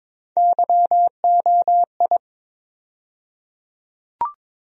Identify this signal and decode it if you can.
Morse code, heard as a keyed tone, about 22 wpm, 700 Hz. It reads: YOI
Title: USA Call Sign Postfixes - 3 Letters 22wpm